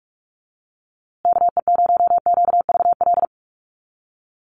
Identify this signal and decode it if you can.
Morse code, heard as a keyed tone, about 44 wpm, 700 Hz. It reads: KE0QVL
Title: USA Call Signs Repeated 2x 45wpm